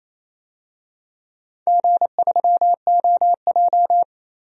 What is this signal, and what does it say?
Morse code, heard as a keyed tone, about 28 wpm, 700 Hz. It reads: G3OJ
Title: DXCC-100 Call Signs - 4 Characters - 28wpm